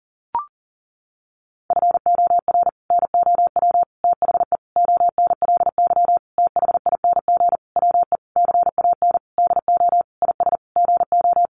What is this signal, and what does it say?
Morse code, heard as a keyed tone, about 40 wpm, 700 Hz. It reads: FOR NOW THE ONLY THING WE CAN DO IS GO
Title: Sentences from Top 200 Words 40wpm